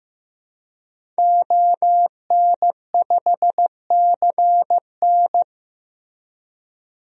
Morse code, heard as a keyed tone, about 15 wpm, 700 Hz. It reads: ON5CN
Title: DXCC-100 Call Signs - 5 Characters - Repeated 3x - 15wpm